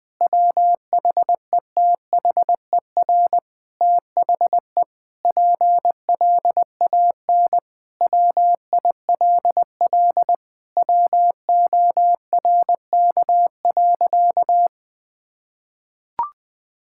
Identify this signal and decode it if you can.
Morse code, heard as a keyed tone, about 20 wpm, 700 Hz. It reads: WHETHER THE PLAN WILL WORK.
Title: Sentences with 7-Word Phrases - 20wpm